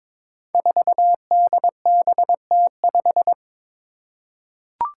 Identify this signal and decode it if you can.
Morse code, heard as a keyed tone, about 22 wpm, 700 Hz. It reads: 4DBT5
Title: Five Challenging Letter-Number Combinations 22wpm